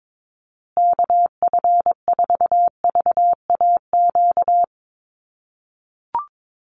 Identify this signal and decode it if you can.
Morse code, heard as a keyed tone, about 22 wpm, 700 Hz. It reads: KF4VAQ
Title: USA Call Signs Repeated 3x 22wpm